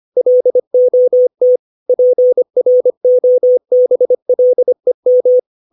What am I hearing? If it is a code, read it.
Morse code, heard as a keyed tone, about 25 wpm, 500 Hz. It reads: LOT PROBLEM
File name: 632 words / 632 Words 25WPM